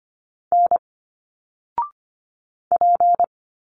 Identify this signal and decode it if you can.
Morse code, heard as a keyed tone, about 25 wpm, 700 Hz. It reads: N P
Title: Single Letters 25wpm